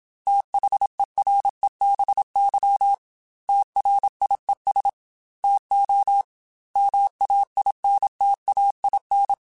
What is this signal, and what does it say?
Morse code, heard as a keyed tone, about 27 wpm, 800 Hz. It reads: THEREBY TRIES TO MAINTAIN